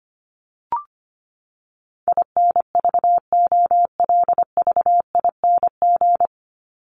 Morse code, heard as a keyed tone, about 25 wpm, 700 Hz. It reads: INVOLVING
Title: ING Suffix - Top 100 Words with Base Words 25wpm